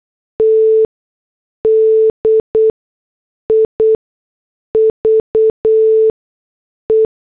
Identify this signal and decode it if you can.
Morse code, heard as a keyed tone, about 8 wpm, spaced as for 6 wpm, 440 Hz. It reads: T DIVE
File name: morse / headlines